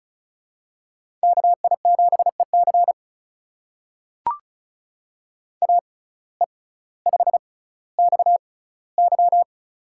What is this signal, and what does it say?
Morse code, heard as a keyed tone, about 36 wpm, spaced as for 18 wpm, 700 Hz. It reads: KI7EC AE5XY
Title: USA Call Signs - 5 Characters - 6x Character Spacing - 35wpm